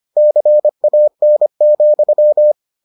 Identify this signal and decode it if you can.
Morse code, heard as a keyed tone, about 25 wpm, 600 Hz. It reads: CAN,